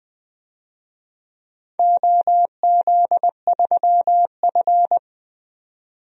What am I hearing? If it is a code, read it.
Morse code, heard as a keyed tone, about 21 wpm, 700 Hz. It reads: OZ3F
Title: DXCC-100 Call Signs - 4 Characters - 20wpm